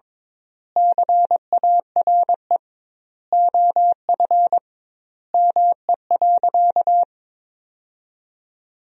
Morse code, heard as a keyed tone, about 21 wpm, 700 Hz. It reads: CARE OF ME.
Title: 2x Word-Spacing - Sentences from Top 300 Words - 22wpm